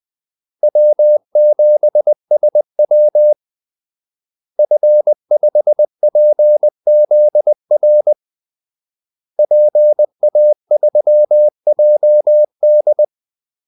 Morse code, heard as a keyed tone, about 20 wpm, 600 Hz. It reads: W7SW F5PZR PA3JD
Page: Kaldesignaler 20 wpm